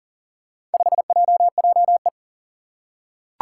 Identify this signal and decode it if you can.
Morse code, heard as a keyed tone, about 40 wpm, 700 Hz. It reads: 5JJE
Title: Four Challenging Letter-Number Combinations 40wpm